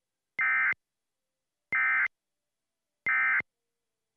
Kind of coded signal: Emergency Alert System (SAME) tones